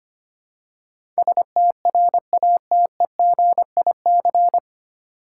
Morse code, heard as a keyed tone, about 25 wpm, 700 Hz. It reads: STRATEGIC
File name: Nine Letter Words - 25wpm